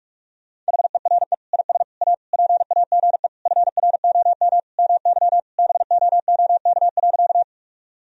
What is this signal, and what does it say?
Morse code, heard as a keyed tone, about 46 wpm, 700 Hz. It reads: HERE IS A PAGE FROM MY BOOK.
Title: Sentences from Top 1000 Words - Review Part 1 - 45wpm